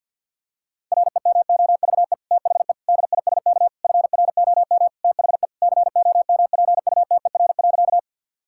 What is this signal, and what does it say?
Morse code, heard as a keyed tone, about 50 wpm, 700 Hz. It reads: REMOVE THE DISK FROM THE COMPUTER.
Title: Sentences from Top 1000 Words - Review Part 6 - 50wpm